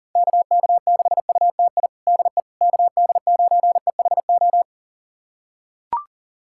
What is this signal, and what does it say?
Morse code, heard as a keyed tone, about 40 wpm, 700 Hz. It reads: KK6UTI DE KD9EHO
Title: USA Callsign DE Callsign 40wpm